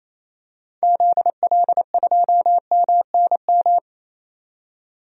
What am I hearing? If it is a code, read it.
Morse code, heard as a keyed tone, about 28 wpm, 700 Hz. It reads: ZL2MNM
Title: DXCC-100 Call Signs - 6 Characters - 28wpm